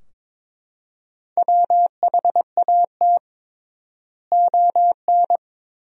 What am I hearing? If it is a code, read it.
Morse code, heard as a keyed tone, about 21 wpm, 700 Hz. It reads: WHAT ON
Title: 3x Word-Spacing - Sentences from Top 200 Words - 22wpm